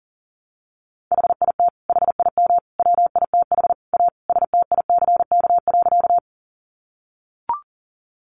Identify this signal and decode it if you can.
Morse code, heard as a keyed tone, about 40 wpm, 700 Hz. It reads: HIT HIM WITH A STICK.
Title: Sentences from Top 1200 Words 40wpm